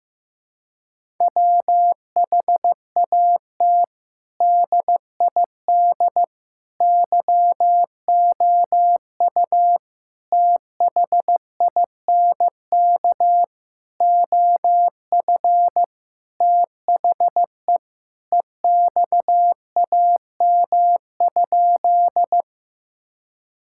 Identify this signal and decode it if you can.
Morse code, heard as a keyed tone, about 15 wpm, 700 Hz. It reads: WHAT DID YOU THINK OF THE EXAM?
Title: Sentences from Top 1000 Words - Review Part 5 - 15wpm